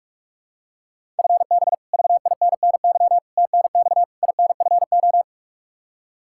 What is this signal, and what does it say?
Morse code, heard as a keyed tone, about 42 wpm, 700 Hz. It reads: FB VINNY TNX INFO
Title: Practice QSO - Exchange 3 - More Info - 45wpm